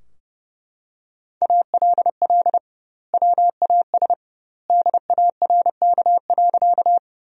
Morse code, heard as a keyed tone, about 30 wpm, 700 Hz. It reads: ALL WAS DARK.
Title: 2x Word-Spacing - Sentences from Top 500 Words - 30wpm